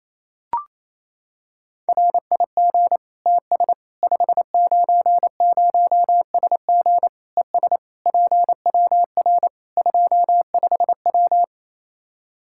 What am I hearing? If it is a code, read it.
Morse code, heard as a keyed tone, about 28 wpm, 700 Hz. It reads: RIG TS 590SG ES PWR 25W
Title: K1USN SST QSO Practice 28wpm